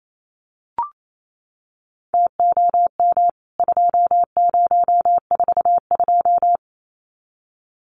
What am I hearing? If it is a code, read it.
Morse code, heard as a keyed tone, about 28 wpm, 700 Hz. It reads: TOM 2042
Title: CWT Exchange 28wpm